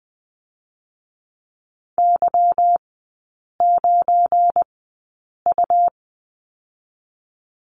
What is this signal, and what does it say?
Morse code, heard as a keyed tone, about 20 wpm, 700 Hz. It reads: Y 9 U
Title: 172 Lesson 28e - Number 9 - Three Characters 20wpm